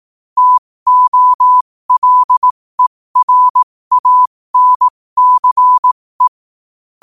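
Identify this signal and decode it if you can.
Morse code, heard as a keyed tone, about 18 wpm, spaced as for 15 wpm, 1000 Hz. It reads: TOLERANCE
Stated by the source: Morse practice sent at 15 wpm.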